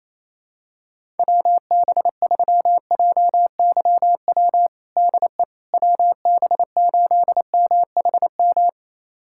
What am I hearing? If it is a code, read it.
Morse code, heard as a keyed tone, about 28 wpm, 700 Hz. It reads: WB3JYW DE WB8MHM
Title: USA Callsign DE Callsign - 6 Characters - 28wpm